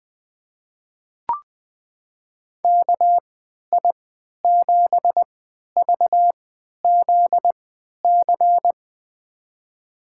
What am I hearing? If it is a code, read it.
Morse code, heard as a keyed tone, about 21 wpm, spaced as for 12 wpm, 700 Hz. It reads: KI7VZC
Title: USA Call Signs - 6 Characters - 3x Character Spacing - 20wpm